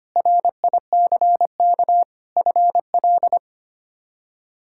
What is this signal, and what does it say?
Morse code, heard as a keyed tone, about 25 wpm, 700 Hz. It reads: RICK FL
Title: K1USN SST QSO Practice - 25wpm